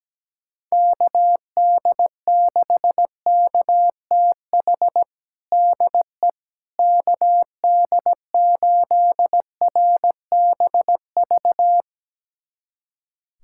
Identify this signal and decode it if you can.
Morse code, heard as a keyed tone, about 17 wpm, 700 Hz. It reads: KD6KTH DE KD8RBV
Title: USA Callsign DE Callsign - 6 Characters - 17wpm